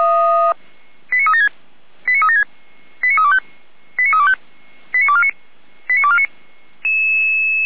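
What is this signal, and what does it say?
Five-tone selective call (ZVEI), heard as quick runs of five tones, tones of about 70 ms. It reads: 89376 89376 89325 89325 89249 89249
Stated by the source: ZVEI five-tone call 89250 received by SDR, feuerwehr-orlamuende.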